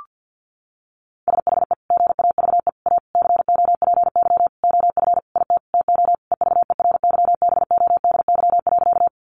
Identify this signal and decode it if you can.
Morse code, heard as a keyed tone, about 50 wpm, 700 Hz. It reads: SHE GAVE A COPY OF IT TO EVERYBODY.
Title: Sentences from Top 1400 Words 50wpm